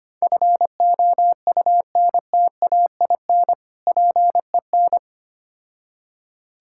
Morse code, heard as a keyed tone, about 25 wpm, 700 Hz. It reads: FOUNTAIN PEN
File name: practice / 25wpm / Difficult Words with Hints - 25wpm